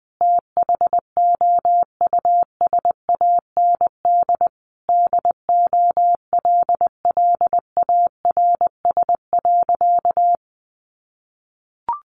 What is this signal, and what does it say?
Morse code, heard as a keyed tone, about 19 wpm, 700 Hz. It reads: THOUSAND DOLLARS.
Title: A Sound Of Thunder 20wpm